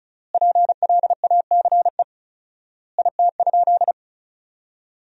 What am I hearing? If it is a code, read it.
Morse code, heard as a keyed tone, about 34 wpm, 700 Hz. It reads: PLACE IT?
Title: 4x Word-Spacing - Sentences from Top 200 Words - 35wpm